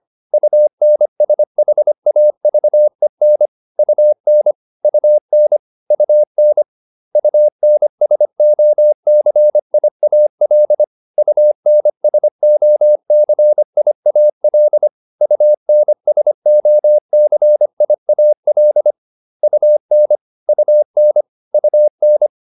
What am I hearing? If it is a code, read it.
Morse code, heard as a keyed tone, about 25 wpm, 600 Hz. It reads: UNSHAVEN UN UN UN UNSOCIAL UNSOCIAL UNSOCIAL UN UN UN